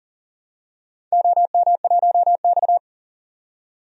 Morse code, heard as a keyed tone, about 42 wpm, 700 Hz. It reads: OM1X
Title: DXCC-100 Call Signs - 4 Characters - 40wpm